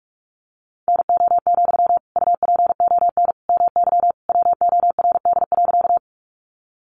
Morse code, heard as a keyed tone, about 44 wpm, 700 Hz. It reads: NO, UPON MY WORD.
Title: Sentences from Top 1100 Words 45wpm